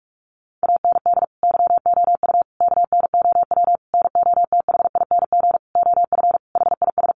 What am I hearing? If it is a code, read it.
Morse code, heard as a keyed tone, about 44 wpm, 700 Hz. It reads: AND YOU KNOW NOTHING OF HIS
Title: Sentences from Top 1200 Words 45wpm